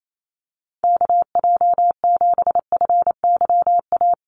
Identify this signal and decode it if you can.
Morse code, heard as a keyed tone, about 28 wpm, 700 Hz. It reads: KJ7FYA
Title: USA Call Signs Repeated 2x 28wpm